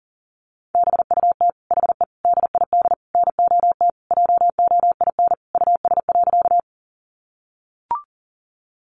Morse code, heard as a keyed tone, about 40 wpm, 700 Hz. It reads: BUT HE DID NOT JOIN US.
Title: Sentences from Top 1400 Words 40wpm